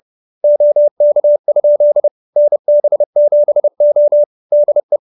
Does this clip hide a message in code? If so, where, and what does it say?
Yes: Morse code from 0.4 to 5.0 s, heard as a keyed tone, about 30 wpm, 570 Hz. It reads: OK? NB7O DE